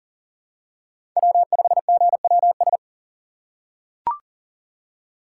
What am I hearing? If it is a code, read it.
Morse code, heard as a keyed tone, about 40 wpm, 700 Hz. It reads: W5GWS
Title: USA Call Signs - 5 Characters - 3x Character Spacing - 40wpm